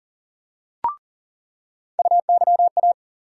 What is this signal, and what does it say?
Morse code, heard as a keyed tone, about 40 wpm, 700 Hz. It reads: UYA
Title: USA Call Sign Postfixes - 3 Letters 40wpm